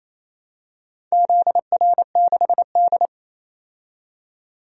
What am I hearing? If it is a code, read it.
Morse code, heard as a keyed tone, about 28 wpm, 700 Hz. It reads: ZR6D
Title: DXCC-100 Call Signs - 4 Characters - 28wpm